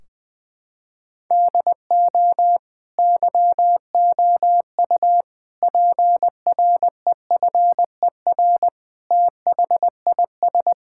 Morse code, heard as a keyed tone, about 20 wpm, 700 Hz. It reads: DO YOU PREFER THIS
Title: Sentences from Top 1600 Words 20wpm